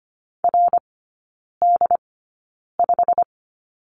Morse code, heard as a keyed tone, about 25 wpm, spaced as for 9 wpm, 690 Hz. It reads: RD5
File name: Lesson 15e - Number 5 - Three Characters 25wpm